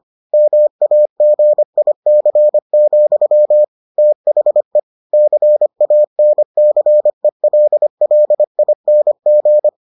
Morse code, heard as a keyed tone, about 25 wpm, 600 Hz. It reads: MAGIC, THE CANCELLING